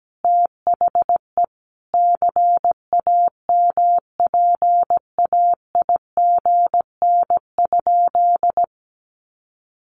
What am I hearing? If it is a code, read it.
Morse code, heard as a keyed tone, about 17 wpm, 700 Hz. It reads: THE CAMPAIGN?
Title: Sentences from Top 1200 Words 17wpm